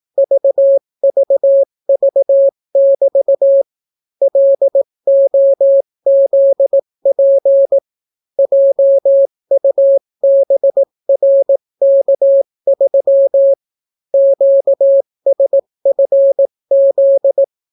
Morse code, heard as a keyed tone, about 18 wpm, 550 Hz. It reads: VVV<BT> LOZP JUBRK3 QSFZ